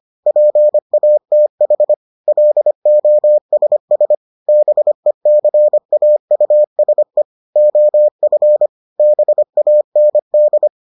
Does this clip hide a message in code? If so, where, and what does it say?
Yes: Morse code from 0.3 to 10.7 s, heard as a keyed tone, about 25 wpm, 600 Hz. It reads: PATH LOSS BECAUSE OF BAND